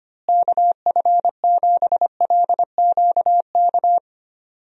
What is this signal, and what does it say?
Morse code, heard as a keyed tone, about 25 wpm, 700 Hz. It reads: KF7LQK
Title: USA Call Signs - 6 Characters - Repeated 3x - 25wpm